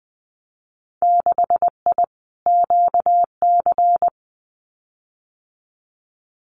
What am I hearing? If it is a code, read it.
Morse code, heard as a keyed tone, about 20 wpm, 700 Hz. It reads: 6I QC
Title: Winter Field Day Exchanges Encore 20wpm